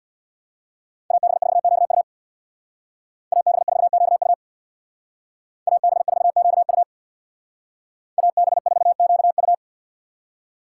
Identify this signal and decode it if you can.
Morse code, heard as a keyed tone, about 62 wpm, 700 Hz. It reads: AB4QU AB4QU AB4QU AB4QU
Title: Speed Racing - USA 2x2 Call Signs 50wpm